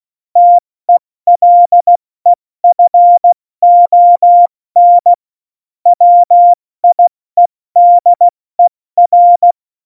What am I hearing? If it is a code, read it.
Morse code, heard as a keyed tone, about 16 wpm, spaced as for 14 wpm, 700 Hz. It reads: TELEFON WIEDER
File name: Morsekurs / Witze / Witz 02. Oktober 2018, 70 F80